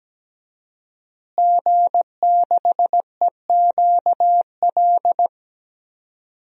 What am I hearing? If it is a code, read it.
Morse code, heard as a keyed tone, about 17 wpm, 700 Hz. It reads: G6EQL
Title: DXCC-100 Call Signs - 5 Characters - Repeated 3x - 17wpm